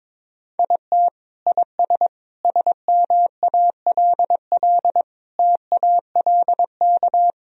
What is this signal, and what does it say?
Morse code, heard as a keyed tone, about 22 wpm, 700 Hz. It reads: IT IS SMALL TALK
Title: Sentences from Top 1000 Words - Review Part 1 - 22wpm